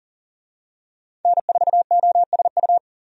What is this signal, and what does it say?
Morse code, heard as a keyed tone, about 40 wpm, 700 Hz. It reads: N4OSU
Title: USA Call Signs - 5 Characters - 40wpm